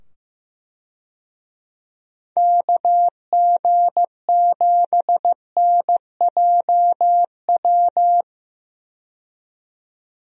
Morse code, heard as a keyed tone, about 15 wpm, 700 Hz. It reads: KG7NJW
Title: USA Call Signs - 6 Characters - 15wpm